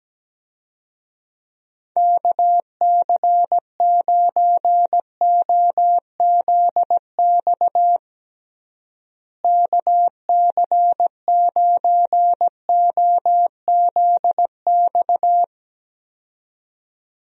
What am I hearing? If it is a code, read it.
Morse code, heard as a keyed tone, about 17 wpm, 700 Hz. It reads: KC9OZX KC9OZX
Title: USA Call Signs - 6 Characters - Repeated 2x - 17wpm